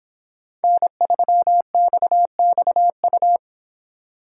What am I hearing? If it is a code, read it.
Morse code, heard as a keyed tone, about 31 wpm, 700 Hz. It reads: N3XXU